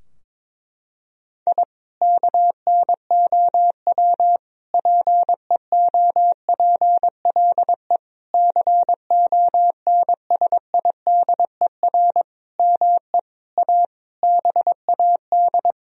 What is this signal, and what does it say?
Morse code, heard as a keyed tone, about 22 wpm, 700 Hz. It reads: I KNOW PEOPLE CONSIDER ME A BAD
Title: Sentences from Top 1000 Words - Review Part 4 - 22wpm